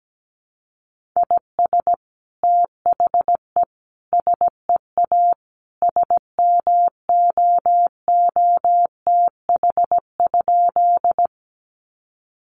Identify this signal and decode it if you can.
Morse code, heard as a keyed tone, about 17 wpm, 700 Hz. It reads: IS THE SEA SMOOTH?
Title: Sentences from Top 1400 Words 17wpm